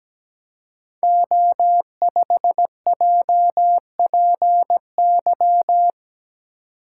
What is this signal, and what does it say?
Morse code, heard as a keyed tone, about 17 wpm, 700 Hz. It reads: O5JPY
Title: Code Groups - 5 Letter:Numbers - 17wpm